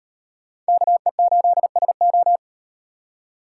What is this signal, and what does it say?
Morse code, heard as a keyed tone, about 35 wpm, 700 Hz. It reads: KE8SO